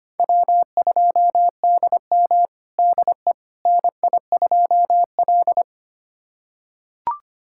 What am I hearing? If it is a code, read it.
Morse code, heard as a keyed tone, about 25 wpm, 700 Hz. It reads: W2DM DE NI2L
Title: USA Callsign DE Callsign - 4 Characters - 25wpm